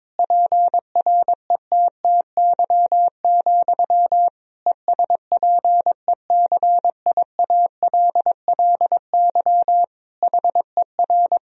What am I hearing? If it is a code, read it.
Morse code, heard as a keyed tone, about 22 wpm, 700 Hz. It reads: PRETTY, ESPECIALLY HER
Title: Sentences from Top 1000 Words - Review Part 4 - 22wpm